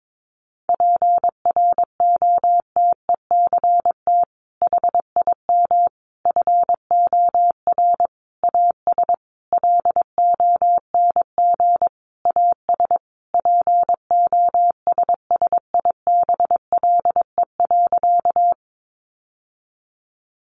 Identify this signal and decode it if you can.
Morse code, heard as a keyed tone, about 22 wpm, 700 Hz. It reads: PROTECT HIM FOR AS LONG AS POSSIBLE.
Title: Sentences from Top 900 Words 22wpm